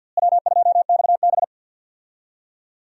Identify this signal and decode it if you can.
Morse code, heard as a keyed tone, about 52 wpm, 700 Hz. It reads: W2XB
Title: USA Call Signs - 4 Characters - Repeated 3x - 50wpm